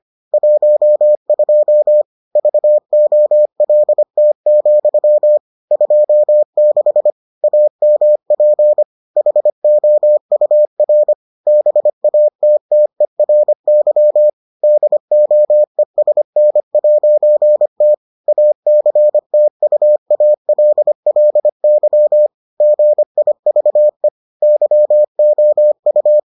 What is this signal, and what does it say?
Morse code, heard as a keyed tone, about 25 wpm, 600 Hz. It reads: 12 VOLT, 26 AMP HOUR BATTERY DOESN'T ACTUALLY GIVE YOU